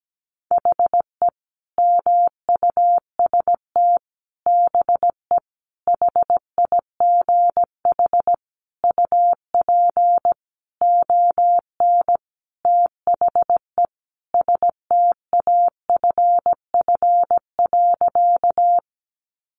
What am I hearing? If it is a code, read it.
Morse code, heard as a keyed tone, about 17 wpm, 700 Hz. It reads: HE MUST BE HIGH UP ON THE STAFF.
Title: Sentences from Top 1200 Words 17wpm